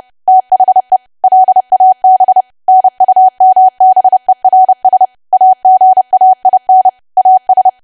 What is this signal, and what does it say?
Morse code, heard as a keyed tone, about 30 wpm, 750 Hz. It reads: THE LAB NUMBERS AGAIN AS
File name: cwa / W1AW-Level3 / 30WPM